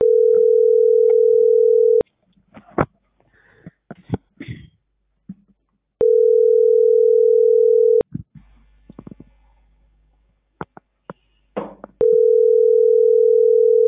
Ringback tone (North America), 440 + 480 Hz.